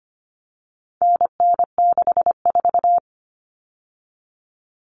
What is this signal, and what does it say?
Morse code, heard as a keyed tone, about 25 wpm, 700 Hz. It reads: NN64
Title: Common Cut Numbers - 4-Digit 25wpm